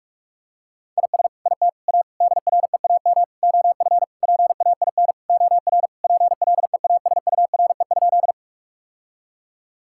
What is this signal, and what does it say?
Morse code, heard as a keyed tone, about 46 wpm, 700 Hz. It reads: IS IT A DREAM OF PAIN OR PLEASURE?
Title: Sentences from Top 1600 Words 45wpm